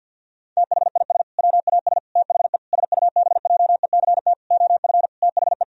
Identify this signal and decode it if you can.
Morse code, heard as a keyed tone, about 52 wpm, 700 Hz. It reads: THIS WAS THE SUBJECT OF THE
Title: Sentences from Top 1000 Words - Review Part 1 - 50wpm